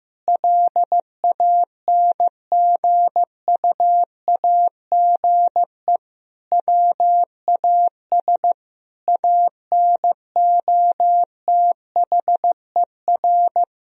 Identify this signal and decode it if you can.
Morse code, heard as a keyed tone, about 15 wpm, 700 Hz. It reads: LANGUAGE WAS ANOTHER
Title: Sentences from Top 1000 Words - Review Part 3 - 15wpm